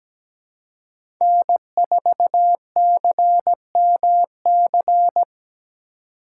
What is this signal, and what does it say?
Morse code, heard as a keyed tone, about 20 wpm, 700 Hz. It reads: N4CMC
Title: Speed Racing - USA 1x3 Call Signs 17wpm